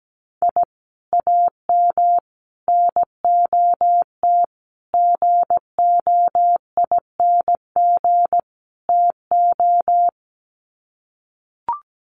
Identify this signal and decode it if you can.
Morse code, heard as a keyed tone, about 17 wpm, 690 Hz. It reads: I AM NOT GOING TO
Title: Sets of 5 Words Encore 17wpm